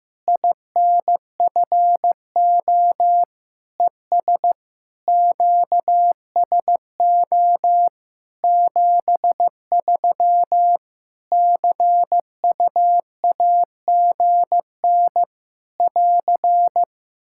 Morse code, heard as a keyed tone, about 15 wpm, 700 Hz. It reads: INFO ES QSO 73 CUAGN <AR>
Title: Practice QSO - Exchange 4 - Ending - 15wpm